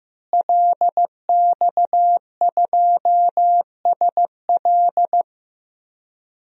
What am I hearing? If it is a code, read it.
Morse code, heard as a keyed tone, about 15 wpm, 700 Hz. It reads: LX2SL
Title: DXCC-100 Call Signs - 5 Characters - Repeated 3x - 15wpm